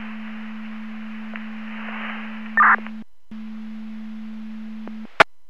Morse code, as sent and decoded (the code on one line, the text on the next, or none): none